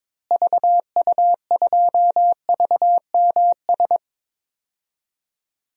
Morse code, heard as a keyed tone, about 22 wpm, 700 Hz. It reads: VU2VMS
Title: DXCC-100 Call Signs - 6 Characters - Repeated 2x - 22wpm